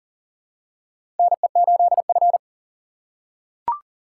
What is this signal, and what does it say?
Morse code, heard as a keyed tone, about 37 wpm, 700 Hz. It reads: NE8F
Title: USA Call Signs - 4 Characters - 2x Character Spacing - 40wpm